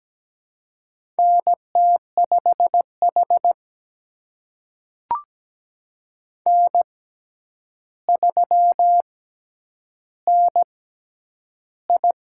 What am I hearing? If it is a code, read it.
Morse code, heard as a keyed tone, about 17 wpm, spaced as for 9 wpm, 700 Hz. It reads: NT5H N3NI
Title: USA Call Signs - 4 Characters - 6x Character Spacing - 17wpm